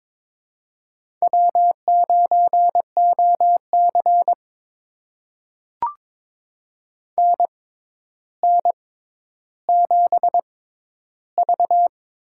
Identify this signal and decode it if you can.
Morse code, heard as a keyed tone, about 22 wpm, spaced as for 11 wpm, 700 Hz. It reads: W9OC NN7V
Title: USA Call Signs - 6x Character Spacing - 22wpm